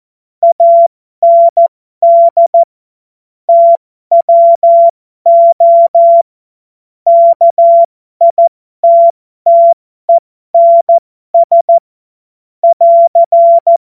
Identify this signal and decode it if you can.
Morse code, heard as a keyed tone, about 14 wpm, spaced as for 12 wpm, 680 Hz. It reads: AND TWO KITTENS <AR>
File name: Morsekurs / Witze